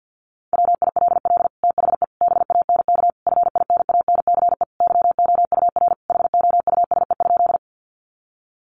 Morse code, heard as a keyed tone, about 50 wpm, 700 Hz. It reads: WILL THE BANK FINANCE YOUR HOUSE?
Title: Sentences from Top 1500 Words 50wpm